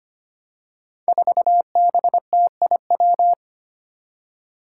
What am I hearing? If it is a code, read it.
Morse code, heard as a keyed tone, about 25 wpm, 700 Hz. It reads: 4BTIW
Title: Five Challenging Letter-Number Combinations 25wpm